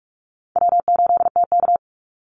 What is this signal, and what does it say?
Morse code, heard as a keyed tone, about 44 wpm, 700 Hz. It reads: W8TK
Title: Easy Call Signs - 45wpm